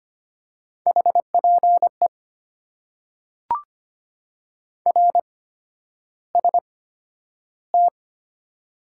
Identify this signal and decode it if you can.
Morse code, heard as a keyed tone, about 25 wpm, spaced as for 11 wpm, 700 Hz. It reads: HPE RST
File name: Top 90 QSO Elements - 8x Character Spacing - 25wpm